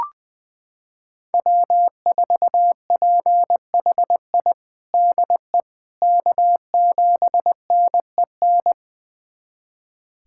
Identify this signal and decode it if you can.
Morse code, heard as a keyed tone, about 20 wpm, 700 Hz. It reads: W4PHI DE K7NEN
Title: USA Callsign DE Callsign - 5 Characters - 20wpm